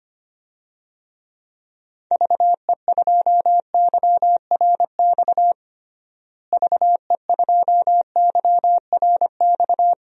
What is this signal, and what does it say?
Morse code, heard as a keyed tone, about 25 wpm, 700 Hz. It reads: VE2YRX VE2YRX
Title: DXCC-100 Call Signs - 6 Characters - Repeated 3x - 25wpm